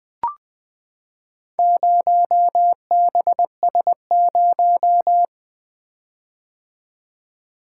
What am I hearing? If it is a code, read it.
Morse code, heard as a keyed tone, about 20 wpm, 700 Hz. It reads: 0BS0
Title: Four Letter-Number Combinations - 20wpm